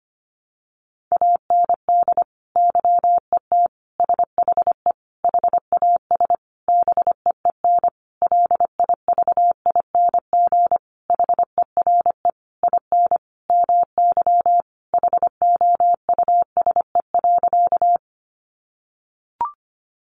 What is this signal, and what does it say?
Morse code, heard as a keyed tone, about 25 wpm, 700 Hz. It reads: AND YET SHE HAS BEEN LIVING HERE IN MY HOUSE.
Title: Sentences from Top 1100 Words 25wpm